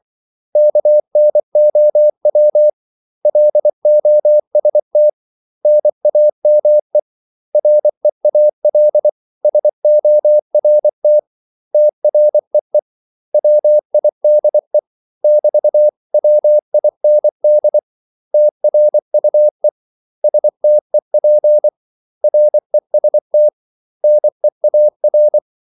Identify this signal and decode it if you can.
Morse code, heard as a keyed tone, about 24 wpm, 600 Hz. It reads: KNOW LOST NAME REAL SORT TREE WIDE <BT>WIND TRUE STEP REST NEAR
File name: CW / Practice / 500MostCommonEnglishWords24